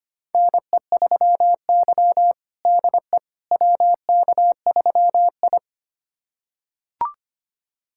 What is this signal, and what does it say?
Morse code, heard as a keyed tone, about 25 wpm, 700 Hz. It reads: NE3Y DE WK3I
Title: USA Callsign DE Callsign - 4 Characters - 25wpm